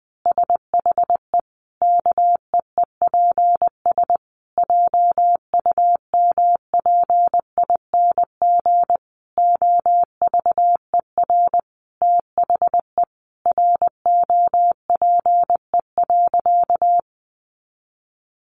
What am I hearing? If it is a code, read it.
Morse code, heard as a keyed tone, about 20 wpm, 700 Hz. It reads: SHE KEEPS JUMPING OVER THE ROPE.
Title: Easy Phrases 20wpm